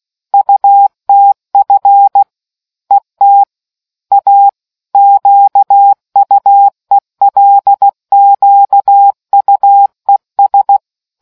Morse code, heard as a keyed tone, about 16 wpm, 800 Hz. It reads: UTF ET A QUELQUES